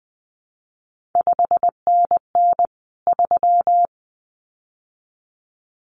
Morse code, heard as a keyed tone, about 20 wpm, 700 Hz. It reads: 5NN 3
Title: CQ WW DX Contest Exchange Encore - 20wpm